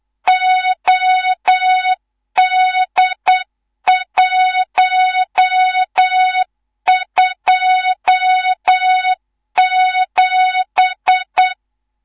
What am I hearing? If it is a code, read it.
Morse code, heard as a keyed tone, about 8 wpm, 3700 Hz, with noise. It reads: OD127
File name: Morze-HA2MN / hang / 19 GYAKORLAT